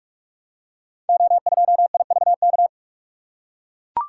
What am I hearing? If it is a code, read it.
Morse code, heard as a keyed tone, about 44 wpm, 700 Hz. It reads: O2IVK
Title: Code Groups - 5 Letter:Numbers - 45wpm